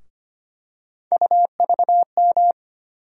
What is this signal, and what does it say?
Morse code, heard as a keyed tone, about 25 wpm, 700 Hz. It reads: UVM
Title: USA Call Sign Postfixes - 3 Letters 25wpm